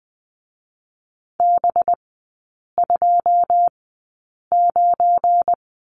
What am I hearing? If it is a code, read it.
Morse code, heard as a keyed tone, about 19 wpm, spaced as for 8 wpm, 690 Hz. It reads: B29
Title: 212 Lesson 34e - Number 8 - Three Characters 20wpm